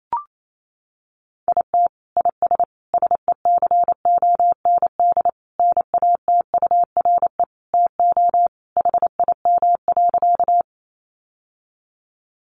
Morse code, heard as a keyed tone, about 28 wpm, 700 Hz. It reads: IT IS SECOND NATURE TO HIM.
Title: Sentences from Top 600 Words 28wpm